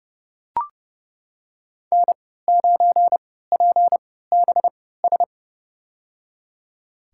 Morse code, heard as a keyed tone, about 32 wpm, spaced as for 17 wpm, 700 Hz. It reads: N9PBS
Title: USA Call Signs - 5 Characters - 3x Character Spacing - 30wpm